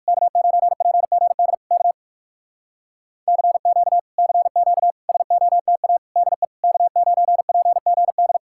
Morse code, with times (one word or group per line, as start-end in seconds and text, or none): 0.1-1.5 K9PGD
1.7-1.9 K
3.3-4.0 CQ
4.2-4.9 CQ
5.1-6.0 SOTA
6.2-6.4 DE
6.6-8.4 K9PGD